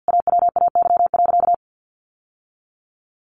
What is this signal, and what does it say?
Morse code, heard as a keyed tone, about 50 wpm, 700 Hz. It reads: AWAY.
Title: Sentences from Top 1000 Words 50wpm